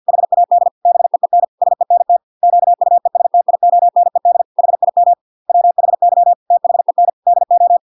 Morse code, heard as a keyed tone, about 50 wpm, 700 Hz. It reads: HAD BEEN SENT QUESTIONED HIM WHY THEN DO